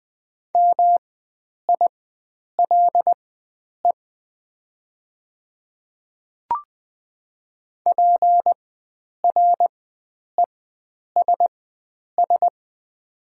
Morse code, heard as a keyed tone, about 20 wpm, spaced as for 9 wpm, 700 Hz. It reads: MILE PRESS
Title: 4x Character Spacing - Top 300 Words - 20wpm